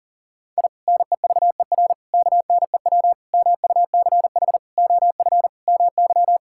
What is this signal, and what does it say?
Morse code, heard as a keyed tone, about 40 wpm, 700 Hz. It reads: I NEVER KNEW MUCH OF MY
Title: Sentences from Top 1000 Words - Review Part 6 - 40wpm